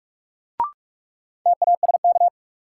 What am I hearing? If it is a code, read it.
Morse code, heard as a keyed tone, about 46 wpm, 700 Hz. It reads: TASK
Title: Sentences from Top 1000 Words - Review Part 7 - New Word Drill - 45wpm